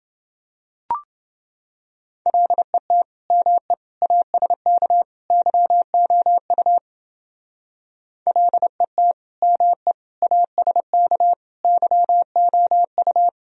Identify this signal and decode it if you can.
Morse code, heard as a keyed tone, about 28 wpm, 700 Hz. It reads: LET ME ASK YOU LET ME ASK YOU